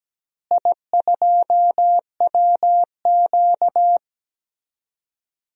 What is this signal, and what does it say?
Morse code, heard as a keyed tone, about 17 wpm, 700 Hz. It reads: I2WQ
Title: DXCC-100 Call Signs - 4 Characters - Repeated 2x - 17wpm